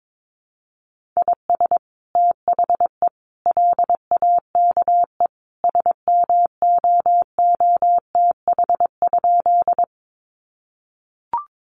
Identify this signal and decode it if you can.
Morse code, heard as a keyed tone, about 22 wpm, 700 Hz. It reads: IS THE LAKE SMOOTH?
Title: Sentences from Top 1400 Words 22wpm